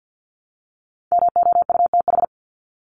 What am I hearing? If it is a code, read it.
Morse code, heard as a keyed tone, about 50 wpm, 700 Hz. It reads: MOUTH
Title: Top 1251-1500 Words 50wpm